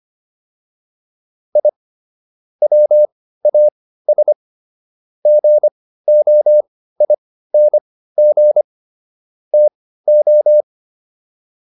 Morse code, heard as a keyed tone, about 25 wpm, spaced as for 15 wpm, 600 Hz. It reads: I WAS GOING TO